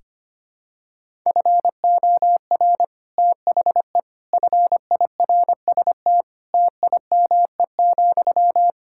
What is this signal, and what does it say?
Morse code, heard as a keyed tone, about 25 wpm, 700 Hz. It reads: FOR THE FIRST TIME,
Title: Sentences from Top 1000 Words - Review Part 6 - 25wpm